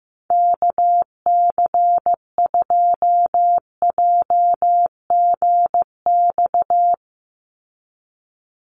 Morse code, heard as a keyed tone, about 15 wpm, 700 Hz. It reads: KC2JGX
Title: USA Call Signs 15wpm